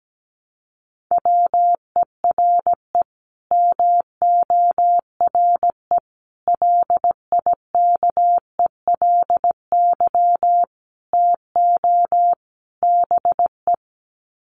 Morse code, heard as a keyed tone, about 17 wpm, 700 Hz. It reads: WERE MORE LIKELY TO BE
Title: Sets of 5 Words Encore2 17wpm